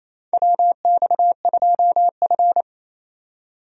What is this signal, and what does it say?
Morse code, heard as a keyed tone, about 28 wpm, 700 Hz. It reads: WX2F
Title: USA Call Signs - 4 Characters - 28wpm